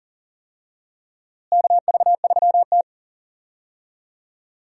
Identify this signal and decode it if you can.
Morse code, heard as a keyed tone, about 40 wpm, 700 Hz. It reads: KV3T
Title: USA Call Signs - 4 Characters - Repeated 2x - 40wpm